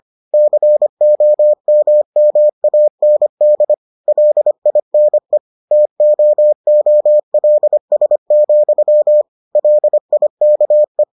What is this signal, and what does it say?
Morse code, heard as a keyed tone, about 25 wpm, 600 Hz. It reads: COMMAND LINE TOOLS, LIKE